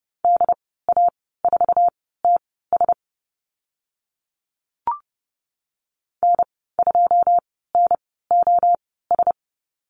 Morse code, heard as a keyed tone, about 31 wpm, spaced as for 17 wpm, 700 Hz. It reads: DA4TS N2NOS
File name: Lesson 15g - Number 5 - Call Signs 30wpm